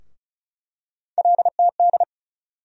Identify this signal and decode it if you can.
Morse code, heard as a keyed tone, about 36 wpm, 700 Hz. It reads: LTD
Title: USA Call Sign Postfixes - 3 Letters 35wpm